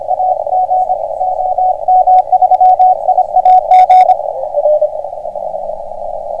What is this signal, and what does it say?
Morse code, heard as a keyed tone, about 26 wpm, 740 Hz, with radio noise. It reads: AG OM3IAG